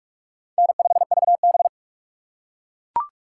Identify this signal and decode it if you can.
Morse code, heard as a keyed tone, about 41 wpm, 700 Hz. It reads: N5VB
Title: USA Call Signs - 4 Characters - 6x Character Spacing - 45wpm